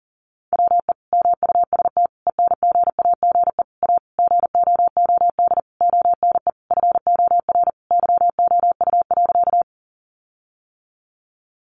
Morse code, heard as a keyed tone, about 40 wpm, 700 Hz. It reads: WE MUST ENGAGE A GOOD ONE FOR YOU.
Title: Sentences from Top 1500 Words 40wpm